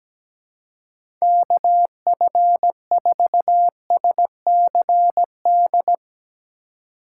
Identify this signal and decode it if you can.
Morse code, heard as a keyed tone, about 17 wpm, 700 Hz. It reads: KF4SCD
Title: USA Call Signs - 6 Characters - 3x Character Spacing - 17wpm